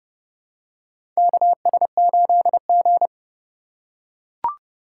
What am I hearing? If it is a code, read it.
Morse code, heard as a keyed tone, about 31 wpm, 700 Hz. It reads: KS8G
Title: USA Call Signs - 4 Characters - 3x Character Spacing - 30wpm